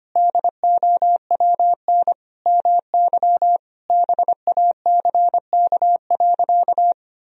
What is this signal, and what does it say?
Morse code, heard as a keyed tone, about 25 wpm, 700 Hz. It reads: DOWN MY BACK.
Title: Sentences from Top 1000 Words - Review Part 3 - 25wpm